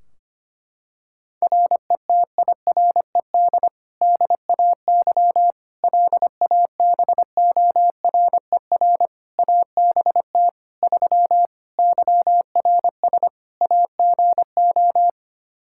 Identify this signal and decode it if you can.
Morse code, heard as a keyed tone, about 25 wpm, 700 Hz. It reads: RETIRED DAY LABORER ABT 3 YRS AGO
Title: Practice QSO - Exchange 3 - More Info - 25wpm